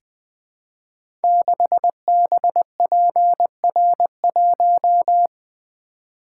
Morse code, heard as a keyed tone, about 20 wpm, 700 Hz. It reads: 6BPR1
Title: Code Groups - 5 Letter:Numbers - 20wpm